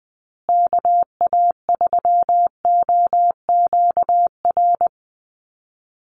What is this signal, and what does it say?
Morse code, heard as a keyed tone, about 20 wpm, 700 Hz. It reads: KA3OQR
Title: USA Call Signs Repeated 3x 20wpm